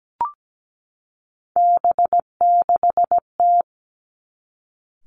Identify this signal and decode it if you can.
Morse code, heard as a keyed tone, about 17 wpm, 700 Hz. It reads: B6T
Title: Three Challenging Letter-Number Combinations 17wpm